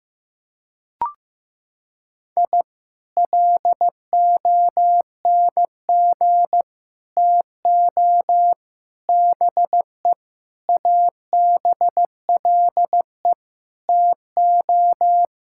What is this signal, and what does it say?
Morse code, heard as a keyed tone, about 15 wpm, 700 Hz. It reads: I LONG TO BE ABLE TO
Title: Sentences from Top 1000 Words - Review Part 6 - 15wpm